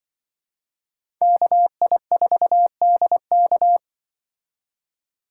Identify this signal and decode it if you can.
Morse code, heard as a keyed tone, about 25 wpm, 700 Hz. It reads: KI4DK
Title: Speed Racing - USA 2x2 Call Signs 20wpm